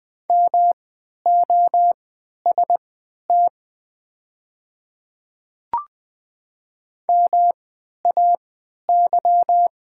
Morse code, heard as a keyed tone, about 20 wpm, spaced as for 11 wpm, 700 Hz. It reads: MOST MAY
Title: 3x Character Spacing - Top 100 Words - 20wpm